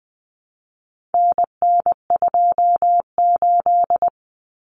Morse code, heard as a keyed tone, about 20 wpm, 700 Hz. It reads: NN28
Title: Common Cut Numbers - 4-Digit 20wpm